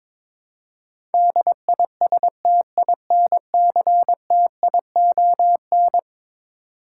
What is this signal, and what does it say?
Morse code, heard as a keyed tone, about 22 wpm, 700 Hz. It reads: DISTINCTION
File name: practice / 22wpm / Eleven Letter Words - 22wpm